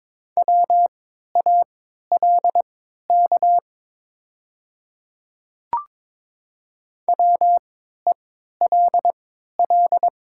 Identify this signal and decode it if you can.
Morse code, heard as a keyed tone, about 23 wpm, spaced as for 13 wpm, 700 Hz. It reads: WALK WELL
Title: 3x Character Spacing - Top 300 Words - 22wpm